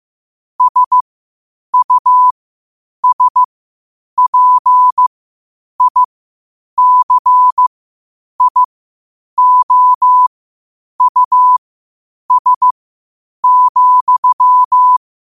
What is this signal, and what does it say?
Morse code, heard as a keyed tone, about 15 wpm, spaced as for 8 wpm, 1000 Hz. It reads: SUSPICIOUS,
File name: qotdfiles / 8wpm